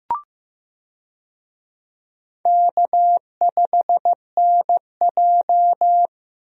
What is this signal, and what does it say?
Morse code, heard as a keyed tone, about 15 wpm, 700 Hz. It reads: K5NJ
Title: USA Call Signs - 4 Characters - Repeated 3x - 15wpm